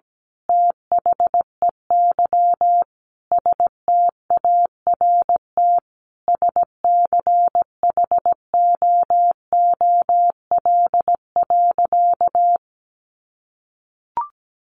Morse code, heard as a keyed tone, about 17 wpm, 700 Hz. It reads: THEY START SCHOOL.
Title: Sentences from Top 1100 Words 17wpm